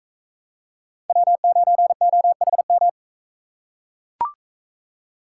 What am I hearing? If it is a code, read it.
Morse code, heard as a keyed tone, about 42 wpm, 700 Hz. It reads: W9OHM